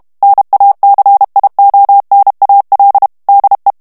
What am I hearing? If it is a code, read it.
Morse code, heard as a keyed tone, about 32 wpm, 800 Hz. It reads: NACIONAL DE
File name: ESPAGNOL TEXTE EN CLAIR 2 VITESSE 30 MOTS MINUTES